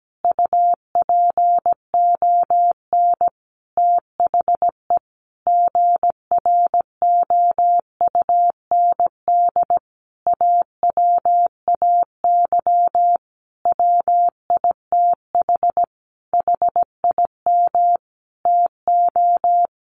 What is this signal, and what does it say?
Morse code, heard as a keyed tone, about 17 wpm, 700 Hz. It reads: UPON THE GROUND AWAY WITH HIM TO
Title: Sentences from Top 1100 Words 17wpm